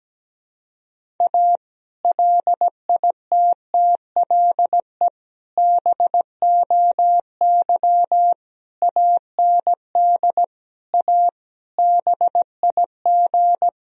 Morse code, heard as a keyed tone, about 17 wpm, 700 Hz. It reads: A LITTLE BOY AND A BIG
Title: Sentences from Top 1000 Words - Review Part 1 - 17wpm